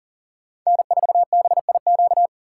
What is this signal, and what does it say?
Morse code, heard as a keyed tone, about 40 wpm, 700 Hz. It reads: N4BIQ
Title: USA Call Signs - 5 Characters - 3x Character Spacing - 40wpm